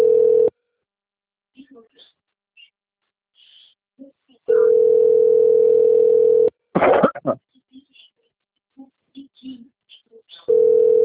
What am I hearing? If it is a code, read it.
Ringback tone (North America), 440 + 480 Hz, with speech.
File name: out- - - - - .63275